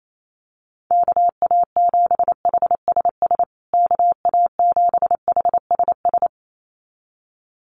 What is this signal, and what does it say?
Morse code, heard as a keyed tone, about 28 wpm, 700 Hz. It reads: KA7HSS KA7HSS
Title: USA Call Signs Repeated 2x 28wpm